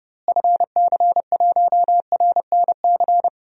Morse code, heard as a keyed tone, about 31 wpm, 700 Hz. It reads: FC1RNC
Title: DXCC-100 Call Signs - 6 Characters - Repeated 3x - 30wpm